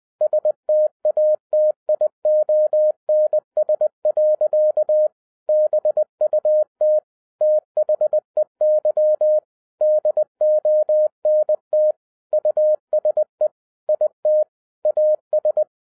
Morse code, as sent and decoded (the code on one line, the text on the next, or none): ... - .- - .. --- -. ... .-.-.- / -... ..- - / - .... -.-- / -.. --- -. - / ..- ... / .. - / .- ...
STATIONS. BUT THEY DONT USE IT AS